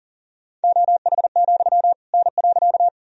Morse code, heard as a keyed tone, about 40 wpm, 700 Hz. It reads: OH, N.
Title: Sentences from Top 500 Words and Abbreviations 40wpm